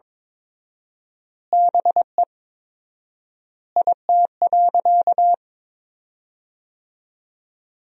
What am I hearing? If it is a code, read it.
Morse code, heard as a keyed tone, about 22 wpm, 700 Hz. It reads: BE IT.
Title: 4x Word-Spacing - Sentences from Top 100 Words - 22wpm